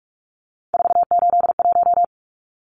Morse code, heard as a keyed tone, about 44 wpm, 700 Hz. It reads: 481
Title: Numbers - 3-Digit 45wpm